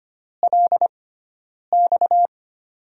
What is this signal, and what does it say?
Morse code, heard as a keyed tone, about 25 wpm, spaced as for 9 wpm, 700 Hz. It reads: LX
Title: USA Call Signs - 6x Character Spacing - 25wpm